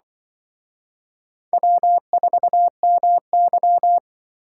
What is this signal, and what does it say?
Morse code, heard as a keyed tone, about 23 wpm, 700 Hz. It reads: W4MY
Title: Speed Racing - USA 1x2 Call Signs 20wpm